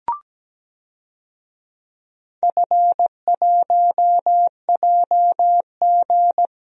Morse code, heard as a keyed tone, about 17 wpm, 700 Hz. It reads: F1JG
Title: DXCC-100 Call Signs - 4 Characters - Repeated 3x - 17wpm